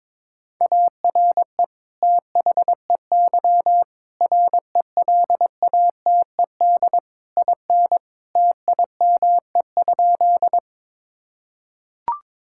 Morse code, heard as a keyed tone, about 22 wpm, 700 Hz. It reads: ARE THEY RELATED IN TIME?
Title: Sentences from Top 1000 Words - Review Part 5 - 22wpm